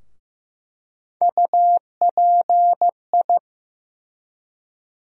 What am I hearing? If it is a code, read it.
Morse code, heard as a keyed tone, about 15 wpm, 700 Hz. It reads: UPI
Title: USA Call Sign Postfixes - 3 Letters 15wpm